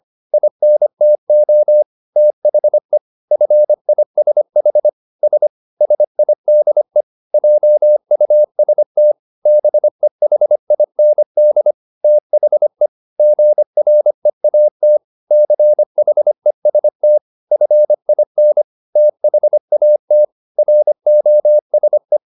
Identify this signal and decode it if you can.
Morse code, heard as a keyed tone, about 25 wpm, 600 Hz. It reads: INTO THE FISH S SIDE JUST BEHIND THE GREAT CHEST FIN THAT ROSE